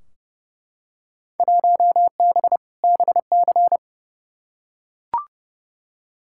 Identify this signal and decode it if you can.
Morse code, heard as a keyed tone, about 32 wpm, 700 Hz. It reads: 1B BC
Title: Field Day Exchanges - 30wpm